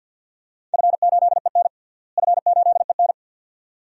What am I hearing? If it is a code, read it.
Morse code, heard as a keyed tone, about 48 wpm, 700 Hz. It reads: F8EN F8EN
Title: DXCC-100 Call Signs - 4 Characters - Repeated 2x - 50wpm